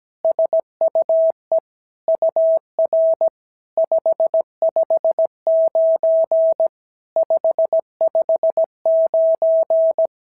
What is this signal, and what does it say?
Morse code, heard as a keyed tone, about 17 wpm, 650 Hz. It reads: SUE UR 559 559
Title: SOTA QSO Practice 17wpm